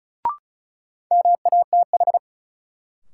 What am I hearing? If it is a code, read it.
Morse code, heard as a keyed tone, about 36 wpm, 700 Hz. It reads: MATH
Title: Sentences from Top 1000 Words - Review Part 6 - New Word Drill - 35wpm